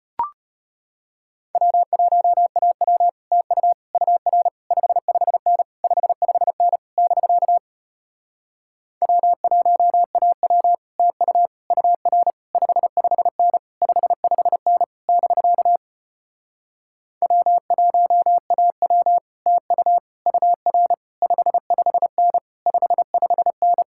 Morse code, heard as a keyed tone, about 33 wpm, 700 Hz. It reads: W1AW TU UR 55N 55N <BK> W1AW TU UR 55N 55N <BK> W1AW TU UR 55N 55N